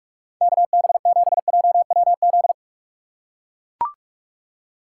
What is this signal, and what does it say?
Morse code, heard as a keyed tone, about 41 wpm, 700 Hz. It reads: KB7JWZ
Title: USA Call Signs - 6 Characters - 2x Character Spacing - 45wpm